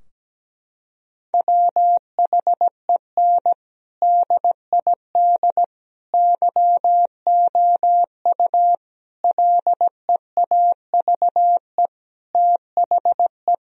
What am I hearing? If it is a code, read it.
Morse code, heard as a keyed tone, about 17 wpm, 700 Hz. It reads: WHEN DID YOU LEAVE THE
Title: Sentences from Top 1000 Words - Review Part 4 - 17wpm